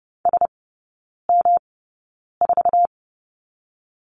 Morse code, heard as a keyed tone, about 30 wpm, spaced as for 9 wpm, 700 Hz. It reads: SM4
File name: Lesson 18e - Letter m - Three Characters 30wpm